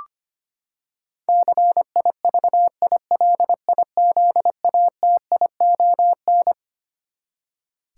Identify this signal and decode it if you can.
Morse code, heard as a keyed tone, about 25 wpm, 700 Hz. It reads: CIVILIZATION
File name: Twelve Letter Words - 25wpm